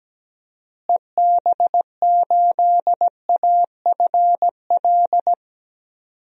Morse code, heard as a keyed tone, about 17 wpm, 700 Hz. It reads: EB8AFL
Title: DXCC-100 Call Signs - 6 Characters - Repeated 2x - 17wpm